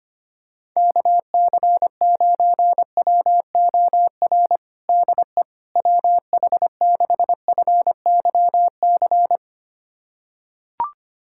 Morse code, heard as a keyed tone, about 25 wpm, 700 Hz. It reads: KC9WOR DE WH6FYC
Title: USA Callsign DE Callsign 25wpm